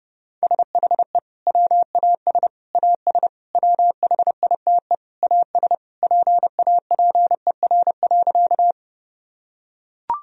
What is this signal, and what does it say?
Morse code, heard as a keyed tone, about 32 wpm, 700 Hz. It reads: SHE WAS AS WHITE AS PAPER.
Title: Sentences from Top 1000 Words - Review Part 1 - 30wpm